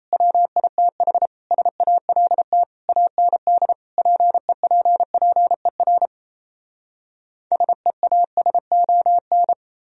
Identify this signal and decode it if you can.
Morse code, heard as a keyed tone, about 38 wpm, 700 Hz. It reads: WITH SALT AND PEPPER SEASON